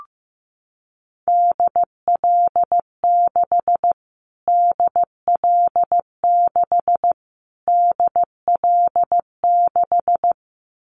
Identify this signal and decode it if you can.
Morse code, heard as a keyed tone, about 15 wpm, 700 Hz. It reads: DL6 DL6 DL6
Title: Callsign Prefixes for Top 25 DXCC - Repeated 3x - 15wpm